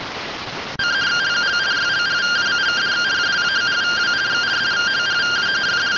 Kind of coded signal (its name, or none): two-tone teleprinter signal